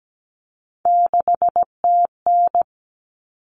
Morse code, heard as a keyed tone, about 17 wpm, 700 Hz. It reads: 6TN
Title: Common Cut Numbers - 3-Digit 17wpm